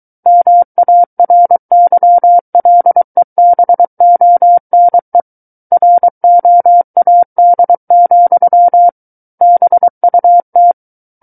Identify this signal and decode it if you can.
Morse code, heard as a keyed tone, about 23 wpm, 700 Hz. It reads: MARYLEBONE ROAD, BUT
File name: Book One / 14-Chapter 14 - 23 WPM